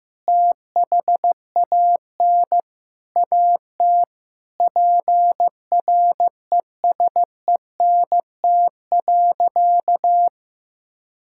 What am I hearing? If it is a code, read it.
Morse code, heard as a keyed tone, about 15 wpm, 700 Hz. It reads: THAN AT PRESENT.
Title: Sentences from Top 1000 Words - Review Part 5 - 15wpm